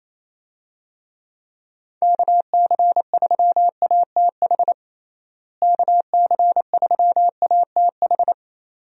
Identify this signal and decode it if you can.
Morse code, heard as a keyed tone, about 28 wpm, 700 Hz. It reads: KC3ATH KC3ATH
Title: USA Call Signs - 6 Characters - Repeated 2x - 28wpm